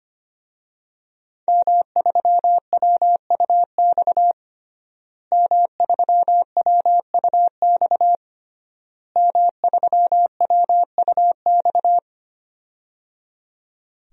Morse code, heard as a keyed tone, about 25 wpm, 700 Hz. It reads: M3WUX M3WUX M3WUX
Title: DXCC-100 Call Signs - 5 Characters - Repeated 3x - 25wpm